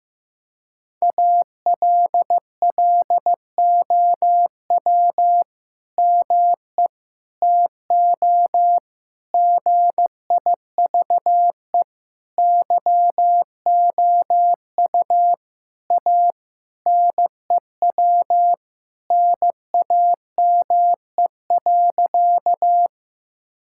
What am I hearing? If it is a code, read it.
Morse code, heard as a keyed tone, about 15 wpm, 700 Hz. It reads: ALLOW ME TO GIVE YOU A NEW NAME.
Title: Sentences from Top 1000 Words - Review Part 5 - 15wpm